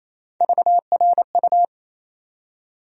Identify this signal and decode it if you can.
Morse code, heard as a keyed tone, about 28 wpm, 700 Hz. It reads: VRU
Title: USA Call Sign Postfixes - 3 Letters 28wpm